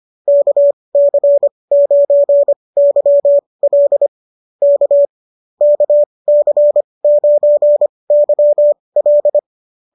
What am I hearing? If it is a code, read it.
Morse code, heard as a keyed tone, about 25 wpm, spaced as for 20 wpm, 590 Hz. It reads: KC9YL K KC9YL